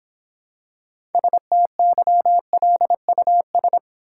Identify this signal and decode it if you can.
Morse code, heard as a keyed tone, about 26 wpm, 700 Hz. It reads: STYLUS